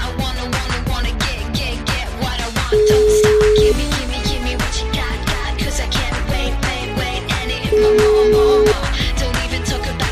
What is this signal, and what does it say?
Ringback tone (Europe, 425 Hz), with noise.